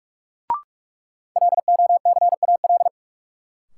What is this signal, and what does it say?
Morse code, heard as a keyed tone, about 47 wpm, 700 Hz. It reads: LOCAL
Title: Sentences from Top 1500 Words - New Word Drill Alt - 45wpm